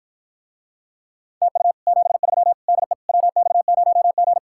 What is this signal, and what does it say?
Morse code, heard as a keyed tone, about 54 wpm, 700 Hz. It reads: TU 73 DE WX0G